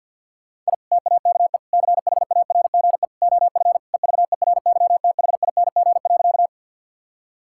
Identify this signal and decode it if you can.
Morse code, heard as a keyed tone, about 50 wpm, 700 Hz. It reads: I TAKE CHARGE OF EVERYTHING.
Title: Sentences from Top 1000 Words - Review Part 6 - 50wpm